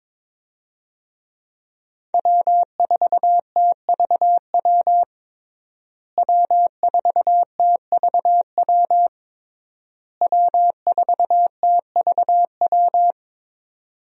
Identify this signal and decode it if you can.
Morse code, heard as a keyed tone, about 22 wpm, 700 Hz. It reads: W4TVW W4TVW W4TVW
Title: USA Call Signs - 5 Characters - Repeated 3x - 22wpm